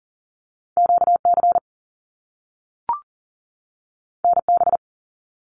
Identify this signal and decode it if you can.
Morse code, heard as a keyed tone, about 41 wpm, 700 Hz. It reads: QC NB
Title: Canadian Provinces and Territories 40wpm